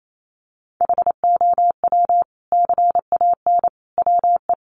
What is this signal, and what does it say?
Morse code, heard as a keyed tone, about 28 wpm, 700 Hz. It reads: HOW CAN WE
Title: Sentences from Top 1400 Words 28wpm